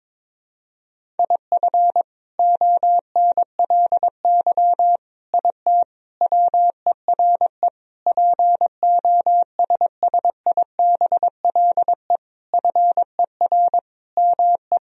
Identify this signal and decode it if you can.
Morse code, heard as a keyed tone, about 22 wpm, 700 Hz. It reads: IF ONLY IT WERE POSSIBLE FER ME
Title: Sentences from Top 500 Words and Abbreviations 22wpm